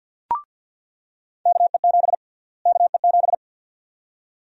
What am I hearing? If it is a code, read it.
Morse code, heard as a keyed tone, about 48 wpm, 700 Hz. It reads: KE7 KE7
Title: USA Call Sign Prefixes - Repeated 2x - 50wpm